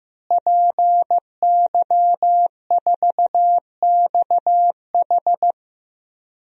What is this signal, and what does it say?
Morse code, heard as a keyed tone, about 15 wpm, 700 Hz. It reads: PY4XH
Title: DXCC-100 Call Signs - 5 Characters - Repeated 3x - 15wpm